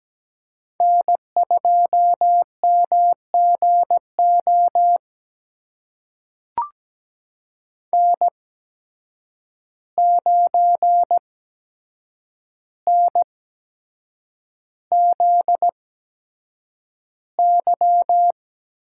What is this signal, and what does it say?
Morse code, heard as a keyed tone, about 17 wpm, spaced as for 7 wpm, 700 Hz. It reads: N2MGO N9NZY
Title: USA Call Signs - 8x Character Spacing - 17wpm